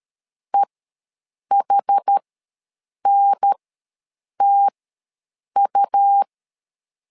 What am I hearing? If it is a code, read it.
Morse code, heard as a keyed tone, about 13 wpm, spaced as for 7 wpm, 800 Hz. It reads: EHNTU
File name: LEZIONI / 2 UHDN / 2 UHDN ws 12wpm 1708